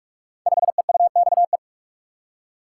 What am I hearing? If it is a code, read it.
Morse code, heard as a keyed tone, about 48 wpm, 700 Hz. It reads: 5EUXE
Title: Code Groups - 5 Letter:Numbers - 45wpm